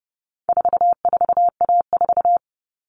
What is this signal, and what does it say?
Morse code, heard as a keyed tone, about 30 wpm, 700 Hz. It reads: 44A4
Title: All Cut Numbers - 4-Digit 30wpm